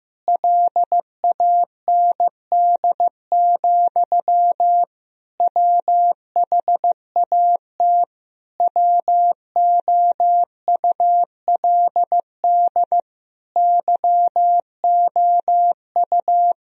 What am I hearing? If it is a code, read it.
Morse code, heard as a keyed tone, about 15 wpm, 700 Hz. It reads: LAND, WHAT WOULD YOU
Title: Sentences from Top 1000 Words - Review Part 1 - 15wpm